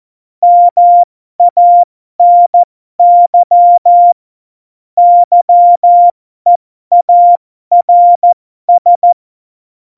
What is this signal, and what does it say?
Morse code, heard as a keyed tone, about 14 wpm, spaced as for 12 wpm, 700 Hz. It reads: MANY YEARS
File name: Morsekurs / Witze